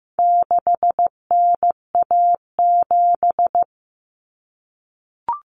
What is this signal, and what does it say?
Morse code, heard as a keyed tone, about 15 wpm, 700 Hz. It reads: 6NA7
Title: Common Cut Numbers - 4-Digit 15wpm